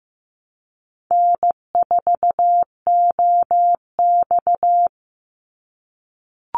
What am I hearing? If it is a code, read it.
Morse code, heard as a keyed tone, about 15 wpm, 700 Hz. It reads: N4OX
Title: Easy Call Signs - 15wpm